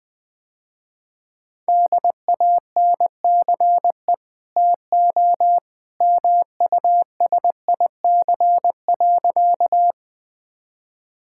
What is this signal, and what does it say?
Morse code, heard as a keyed tone, about 20 wpm, 700 Hz. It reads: DANCE TO MUSIC.
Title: Easy Sentences with Hints - 20wpm